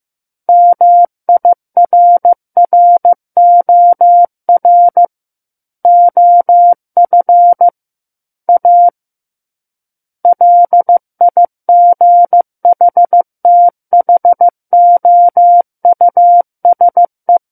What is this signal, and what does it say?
Morse code, heard as a keyed tone, about 15 wpm, 700 Hz. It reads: MIRROR OF A LIGHTHOUSE
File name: Chapter 6 - 15 WPM